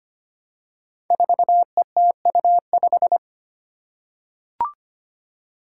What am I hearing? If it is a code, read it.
Morse code, heard as a keyed tone, about 25 wpm, 700 Hz. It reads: 4ETU5
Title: Five Challenging Letter-Number Combinations 25wpm